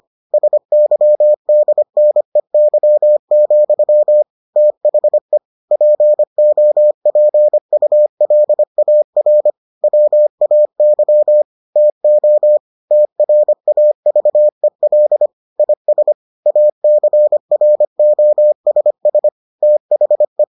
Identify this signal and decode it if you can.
Morse code, heard as a keyed tone, about 25 wpm, 600 Hz. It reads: SYDNEY, THE POPULAR WAY TO TRAVEL IS ACROSS THE